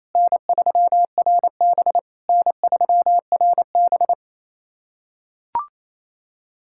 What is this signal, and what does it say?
Morse code, heard as a keyed tone, about 28 wpm, 690 Hz. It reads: N3RB N3RB
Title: SOTA QSO Practice 28wpm